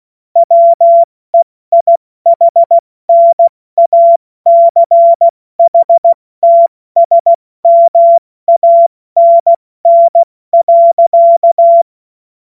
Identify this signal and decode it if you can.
Morse code, heard as a keyed tone, about 16 wpm, spaced as for 14 wpm, 680 Hz. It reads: WEIHNACHTSMANN.
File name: Morsekurs / Witze / Witz 31. Januar 2023 70 F80